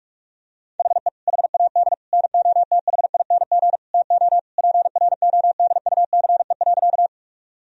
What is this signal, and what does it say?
Morse code, heard as a keyed tone, about 48 wpm, 700 Hz. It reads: HE HAD NOTHING TO PRODUCE.
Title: Sentences from Top 1000 Words - Review Part 3 - 45wpm